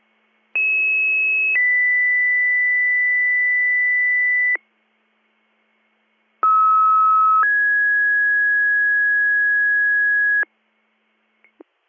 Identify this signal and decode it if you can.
Two-tone page, heard as a one-second tone, then a three-second tone, about 2600 then 2000 Hz, about 1290 then 1650 Hz. It is heard from 0.6 to 10.4 s.